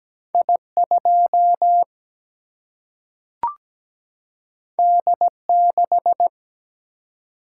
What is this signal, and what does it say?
Morse code, heard as a keyed tone, about 17 wpm, 700 Hz. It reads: I 2 D 6
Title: Single-Letter Single-Number 17wpm